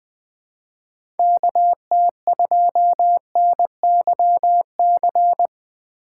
Morse code, heard as a keyed tone, about 21 wpm, 700 Hz. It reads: KT2NYC
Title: USA Call Signs - 6 Characters - 3x Character Spacing - 20wpm